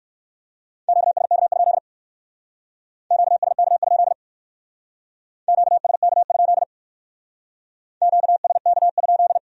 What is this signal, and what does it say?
Morse code, heard as a keyed tone, about 56 wpm, 700 Hz. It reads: QSK? QSK? QSK? QSK?